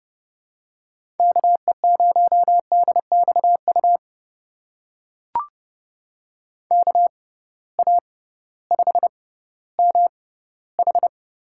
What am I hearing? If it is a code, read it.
Morse code, heard as a keyed tone, about 30 wpm, spaced as for 16 wpm, 700 Hz. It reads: KE0DXU KA5MH
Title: USA Call Signs - 6 Characters - 6x Character Spacing - 30wpm